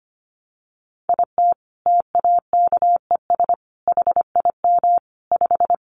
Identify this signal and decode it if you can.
Morse code, heard as a keyed tone, about 25 wpm, 700 Hz. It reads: IT TAKES HIM 5
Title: Additional Easy Phrases 25wpm